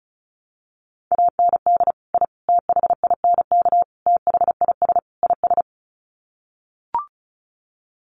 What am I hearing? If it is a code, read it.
Morse code, heard as a keyed tone, about 34 wpm, 700 Hz. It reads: AND I THINK THIS IS
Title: Sets of 5 Words Encore 35wpm